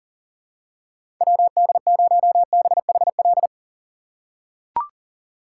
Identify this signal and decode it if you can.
Morse code, heard as a keyed tone, about 40 wpm, 700 Hz. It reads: WD0BHL
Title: USA Call Signs - 2x Character Spacing - 40wpm